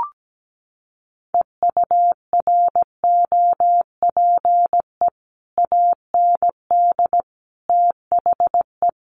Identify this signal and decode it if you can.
Morse code, heard as a keyed tone, about 17 wpm, 700 Hz. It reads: EUROPE AND THE
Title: Sets of 5 Words Encore2 17wpm